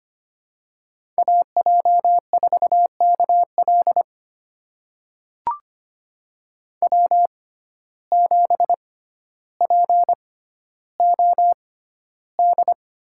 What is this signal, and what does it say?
Morse code, heard as a keyed tone, about 25 wpm, spaced as for 13 wpm, 700 Hz. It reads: AJ4KL W7POD
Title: USA Call Signs - 5 Characters - 6x Character Spacing - 25wpm